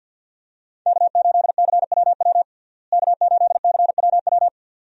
Morse code, heard as a keyed tone, about 52 wpm, 700 Hz. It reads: K8CWW K8CWW
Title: USA Call Signs - 5 Characters - Repeated 2x - 50wpm